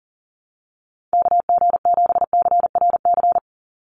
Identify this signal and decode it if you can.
Morse code, heard as a keyed tone, about 38 wpm, 700 Hz. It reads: KG7CRC
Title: USA Call Signs 40wpm